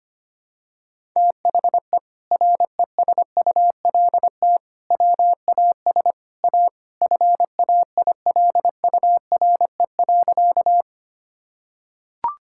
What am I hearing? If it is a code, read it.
Morse code, heard as a keyed tone, about 25 wpm, 700 Hz. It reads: THE RESULT WAS A FAILURE.
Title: Sentences from Top 1000 Words - Review Part 6 - 25wpm